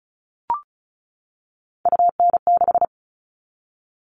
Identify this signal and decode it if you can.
Morse code, heard as a keyed tone, about 34 wpm, 700 Hz. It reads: UN6
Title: All Cut Numbers - 3-Digit 35wpm